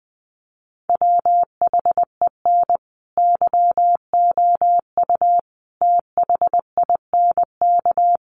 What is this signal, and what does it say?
Morse code, heard as a keyed tone, about 20 wpm, 700 Hz. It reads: WHEN YOU THINK
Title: Sets of 5 Words Encore2 20wpm